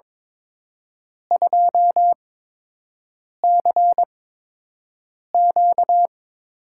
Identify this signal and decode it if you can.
Morse code, heard as a keyed tone, about 23 wpm, spaced as for 6 wpm, 700 Hz. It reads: 2CQ
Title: USA Call Signs - 5 Characters - 8x Character Spacing - 22wpm